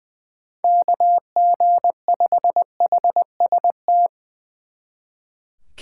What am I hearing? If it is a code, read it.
Morse code, heard as a keyed tone, about 22 wpm, 700 Hz. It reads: KG5HST
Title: Speed Racing - USA Call Signs 20wpm